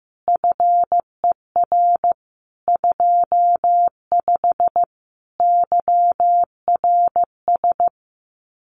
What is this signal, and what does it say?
Morse code, heard as a keyed tone, about 15 wpm, 700 Hz. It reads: FER 25 YRS
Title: Common QSO Phrases Encore 15wpm